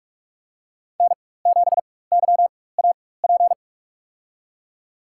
Morse code, heard as a keyed tone, about 46 wpm, spaced as for 21 wpm, 700 Hz. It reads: N7YAP
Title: USA Call Signs - 5 Characters - 4x Character Spacing - 45wpm